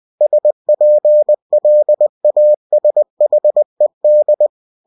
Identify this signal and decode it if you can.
Morse code, heard as a keyed tone, about 20 wpm, 600 Hz. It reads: SPLASHED
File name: cw-books / winnie-the-pooh / winnie-20wpm-page-0057